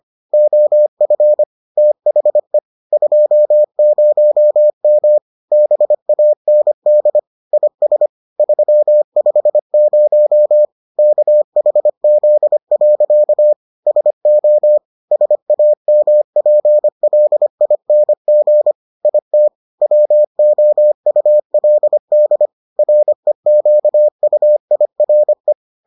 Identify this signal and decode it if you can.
Morse code, heard as a keyed tone, about 25 wpm, 600 Hz. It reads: OF THE 20M BAND IS 350 KHZ. SO SAMPLING IT WOULD REQUIRE